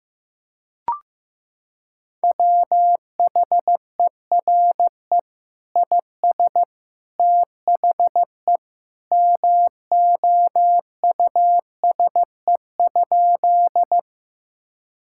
Morse code, heard as a keyed tone, about 15 wpm, 700 Hz. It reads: WHERE IS THE MOUSE?
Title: Sentences from Top 1000 Words - Review Part 5 - 15wpm